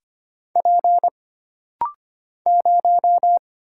Morse code, heard as a keyed tone, about 25 wpm, 700 Hz. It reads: P 0
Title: Single Letter-Number Alt - Rapid-Fire 25wpm